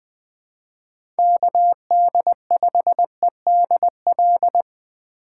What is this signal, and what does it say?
Morse code, heard as a keyed tone, about 20 wpm, 700 Hz. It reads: KD5EDL
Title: 2023 POTA Hunter Call Signs - 20wpm